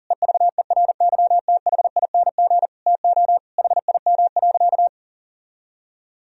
Morse code, heard as a keyed tone, about 40 wpm, 700 Hz. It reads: EVERYTHING TO HIM.
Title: Sentences from Top 1000 Words - Review Part 3 - 40wpm